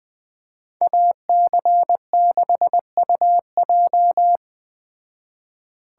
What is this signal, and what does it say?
Morse code, heard as a keyed tone, about 18 wpm, 700 Hz. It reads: AC6UJ
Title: Speed Racing - USA 2x2 Call Signs 17wpm